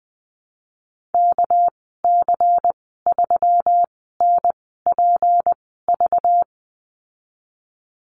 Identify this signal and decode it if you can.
Morse code, heard as a keyed tone, about 20 wpm, spaced as for 15 wpm, 700 Hz. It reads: KC3NPV
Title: 200 Lesson 32g - Letter k - Call Signs 20wpm